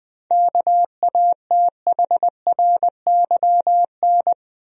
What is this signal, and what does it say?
Morse code, heard as a keyed tone, about 20 wpm, 700 Hz. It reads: KATHRYN
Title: USA Common Names - 20wpm